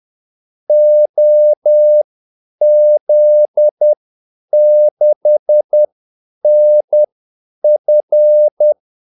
Morse code, heard as a keyed tone, about 10 wpm, spaced as for 8 wpm, 600 Hz. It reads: OZ6NF
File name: Callsigns-10-8wpm